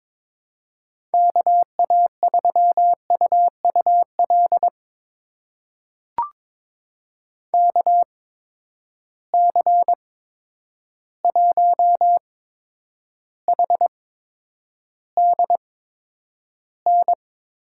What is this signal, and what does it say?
Morse code, heard as a keyed tone, about 22 wpm, spaced as for 9 wpm, 700 Hz. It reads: KA3UUL KC1HDN
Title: USA Call Signs - 8x Character Spacing - 22wpm